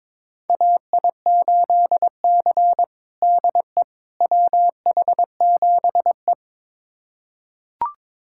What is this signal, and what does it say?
Morse code, heard as a keyed tone, about 22 wpm, 700 Hz. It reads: AI8C DE WH7E
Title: USA Callsign DE Callsign - 4 Characters - 22wpm